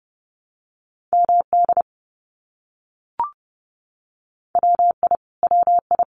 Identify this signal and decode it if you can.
Morse code, heard as a keyed tone, about 30 wpm, 700 Hz. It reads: MD WI WI
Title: US States Repeated 3x 30wpm